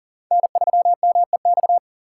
Morse code, heard as a keyed tone, about 41 wpm, 700 Hz. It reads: N3MEX
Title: USA Call Signs - 3x Character Spacing - 40wpm